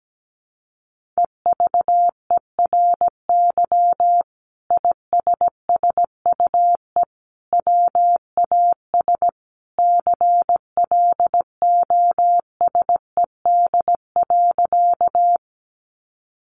Morse code, heard as a keyed tone, about 17 wpm, 700 Hz. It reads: EVERY ISSUE WAS CLOSED.
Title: Sentences from Top 1400 Words 17wpm